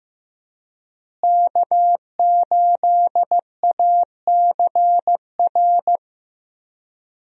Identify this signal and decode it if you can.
Morse code, heard as a keyed tone, about 15 wpm, 700 Hz. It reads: K8ACR
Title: USA Call Signs - 5 Characters - Repeated 2x - 15wpm